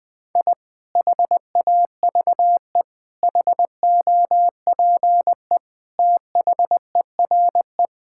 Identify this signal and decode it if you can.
Morse code, heard as a keyed tone, about 20 wpm, 700 Hz. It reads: I HAVE HOPE THERE
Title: Sentences from Top 1000 Words - Review Part 5 - 20wpm